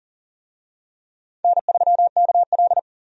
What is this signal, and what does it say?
Morse code, heard as a keyed tone, about 40 wpm, 700 Hz. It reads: N3KL
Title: USA Call Signs - 4 Characters - 40wpm